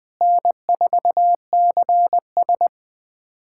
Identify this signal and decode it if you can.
Morse code, heard as a keyed tone, about 20 wpm, 700 Hz. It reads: N4CS
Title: USA Callsign DE Callsign - 4 Characters - 20wpm